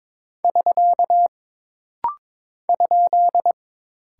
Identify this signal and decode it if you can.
Morse code, heard as a keyed tone, about 22 wpm, 700 Hz. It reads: <SK> ?
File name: practice / 22wpm / Single Punctuation-Prosign - Rapid-Fire - 22wpm